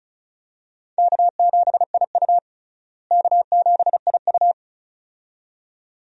Morse code, heard as a keyed tone, about 36 wpm, 700 Hz. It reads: K7IU K7IU
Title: USA Call Signs - 4 Characters - Repeated 2x - 35wpm